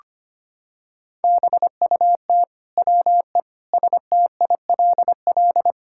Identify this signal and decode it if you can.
Morse code, heard as a keyed tone, about 25 wpm, 700 Hz. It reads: BUT WE STILL
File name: Sets of 9 Words - 25wpm